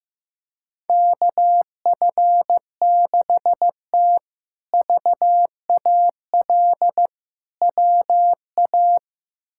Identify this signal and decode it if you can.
Morse code, heard as a keyed tone, about 15 wpm, 700 Hz. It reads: KF6T VAL WA
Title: K1USN SST QSO Practice - 15wpm